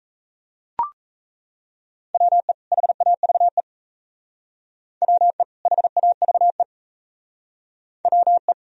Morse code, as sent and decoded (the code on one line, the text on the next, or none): .-- . / .... .- ...- . / .-- . / .... .- ...- . / .-- .
WE HAVE WE HAVE WE